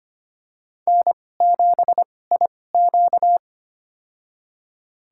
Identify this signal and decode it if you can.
Morse code, heard as a keyed tone, about 25 wpm, spaced as for 18 wpm, 700 Hz. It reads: N7IQ
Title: USA Call Signs - 4 Characters - 2x Character Spacing - 25wpm